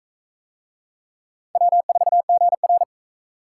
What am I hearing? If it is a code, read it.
Morse code, heard as a keyed tone, about 35 wpm, 700 Hz. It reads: W4GR
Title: Speed Racing - USA 1x2 Call Signs 28wpm